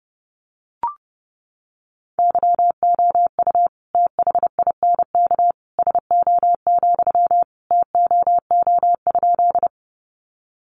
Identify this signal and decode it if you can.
Morse code, heard as a keyed tone, about 30 wpm, 700 Hz. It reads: YOU THINK SO, TOO?
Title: Sentences from Top 200 Words 30wpm